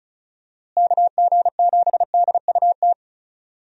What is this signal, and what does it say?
Morse code, heard as a keyed tone, about 34 wpm, 700 Hz. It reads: KG7DUT
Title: USA Call Signs - 6 Characters - Repeated 3x - 35wpm